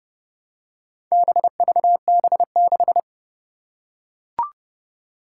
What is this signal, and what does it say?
Morse code, heard as a keyed tone, about 30 wpm, 700 Hz. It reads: BVB6
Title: Four Challenging Letter-Number Combinations 30wpm